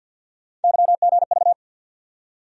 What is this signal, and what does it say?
Morse code, heard as a keyed tone, about 53 wpm, 700 Hz. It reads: YGV
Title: USA Call Sign Postfixes - 3 Letters 50wpm